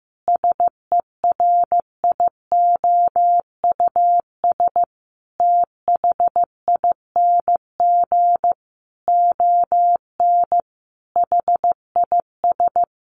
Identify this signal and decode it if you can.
Morse code, heard as a keyed tone, about 15 wpm, 700 Hz. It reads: SERIOUS THING ON HIS
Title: Sentences from Top 1100 Words 15wpm